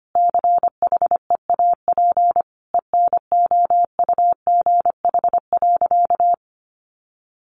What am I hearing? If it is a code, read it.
Morse code, heard as a keyed tone, about 25 wpm, 700 Hz. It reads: CHEAP ENOUGH.
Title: Sentences from Top 1200 Words 25wpm